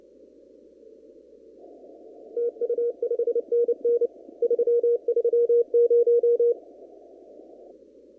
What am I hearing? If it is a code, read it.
Morse code, heard as a keyed tone, about 29 wpm, 470 Hz. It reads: TU 5NN 330
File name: CW / kt5j wpx cw-op-k5pi